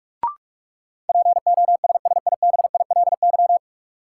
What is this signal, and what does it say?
Morse code, heard as a keyed tone, about 46 wpm, 700 Hz. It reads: POSSIBILY
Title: Sentences from Top 1300 Words - New Word Drill - 45wpm